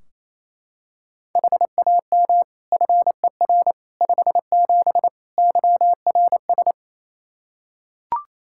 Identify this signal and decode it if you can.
Morse code, heard as a keyed tone, about 28 wpm, 700 Hz. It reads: HAM FER 57 YRS
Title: Practice QSO - Exchange 3 - More Info - 28wpm